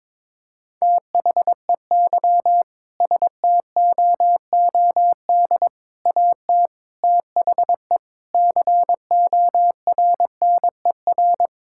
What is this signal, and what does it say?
Morse code, heard as a keyed tone, about 22 wpm, 700 Hz. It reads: THEY STOOD AT THE CORNER
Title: Sentences from Top 1600 Words 22wpm